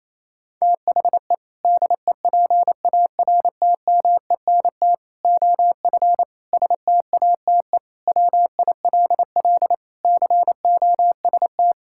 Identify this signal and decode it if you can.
Morse code, heard as a keyed tone, about 27 wpm, 700 Hz. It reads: THE DEPARTMENT OF STATE WILL COST
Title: Sentences from Top 1000 Words - Review Part 5 - 28wpm